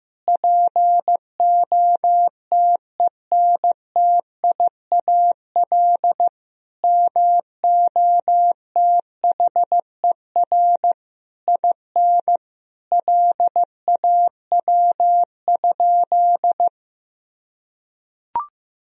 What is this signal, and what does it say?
Morse code, heard as a keyed tone, about 15 wpm, 700 Hz. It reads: POTENTIAL MOTHER IN LAW?
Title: Sentences from Top 1000 Words - Review Part 4 - 15wpm